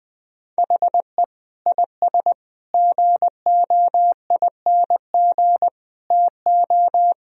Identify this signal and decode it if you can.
Morse code, heard as a keyed tone, about 21 wpm, 700 Hz. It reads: HE IS GOING TO
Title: Sentences from Top 1000 Words - Review Part 4 - 20wpm